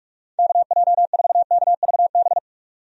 Morse code, heard as a keyed tone, about 46 wpm, 700 Hz. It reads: KJ4KVB
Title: USA Call Signs - 6 Characters - 45wpm